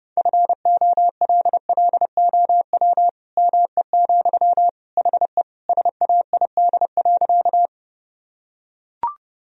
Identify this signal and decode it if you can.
Morse code, heard as a keyed tone, about 30 wpm, 700 Hz. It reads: FOLLOW ME, HE SAID.
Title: Sentences from Top 1000 Words - Review Part 1 - 30wpm